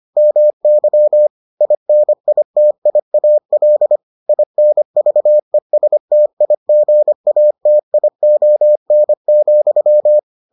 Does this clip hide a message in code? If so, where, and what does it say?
Yes: Morse code from 0.2 to 10.2 s, heard as a keyed tone, about 25 wpm, 600 Hz. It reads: MY INITIAL INVESTIGATION,